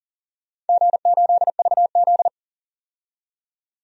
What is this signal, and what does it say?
Morse code, heard as a keyed tone, about 40 wpm, 700 Hz. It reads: G8VZ
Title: DXCC-100 Call Signs - 4 Characters - 40wpm